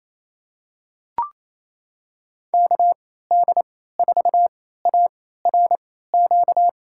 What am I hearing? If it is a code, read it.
Morse code, heard as a keyed tone, about 28 wpm, spaced as for 16 wpm, 700 Hz. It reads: KD4ARQ
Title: USA Call Signs - 6 Characters - 3x Character Spacing - 28wpm